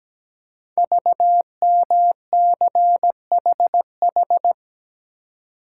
Morse code, heard as a keyed tone, about 17 wpm, 700 Hz. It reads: VMCHH
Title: Code Groups - 5 Letter:Numbers - 17wpm